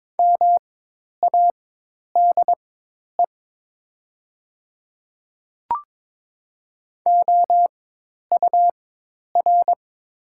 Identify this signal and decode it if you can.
Morse code, heard as a keyed tone, about 22 wpm, spaced as for 10 wpm, 700 Hz. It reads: MADE OUR
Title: 4x Character Spacing - Top 500 Words - 22wpm